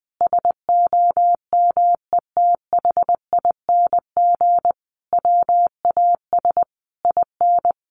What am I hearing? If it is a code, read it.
Morse code, heard as a keyed tone, about 20 wpm, 700 Hz. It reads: SOMETHING WAS IN
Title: Sentences from Top 1400 Words 20wpm